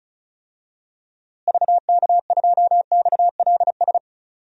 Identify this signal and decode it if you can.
Morse code, heard as a keyed tone, about 35 wpm, 700 Hz. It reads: VK2XLS
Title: DXCC-100 Call Signs - 6 Characters - Repeated 3x - 35wpm